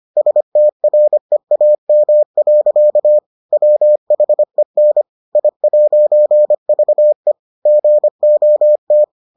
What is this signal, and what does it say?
Morse code, heard as a keyed tone, about 25 wpm, 600 Hz. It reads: STREAM. WHEN I'VE GOT